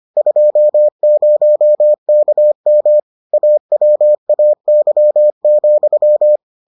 Morse code, heard as a keyed tone, about 25 wpm, 600 Hz. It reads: 20KM AWAY,